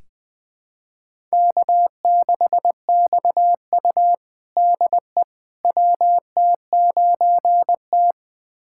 Morse code, heard as a keyed tone, about 21 wpm, 700 Hz. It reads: K6XU DE WT9T
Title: USA Callsign DE Callsign - 4 Characters - 20wpm